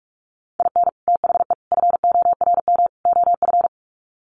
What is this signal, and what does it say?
Morse code, heard as a keyed tone, about 44 wpm, 700 Hz. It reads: IN THE FORM OF
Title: Sets of 4 Words 45wpm